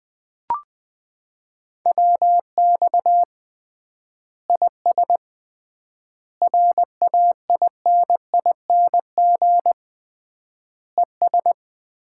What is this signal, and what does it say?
Morse code, heard as a keyed tone, about 21 wpm, 700 Hz. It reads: WX IS RAINING ES
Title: Common QSO Phrases - 3x Word-Spacing - 20wpm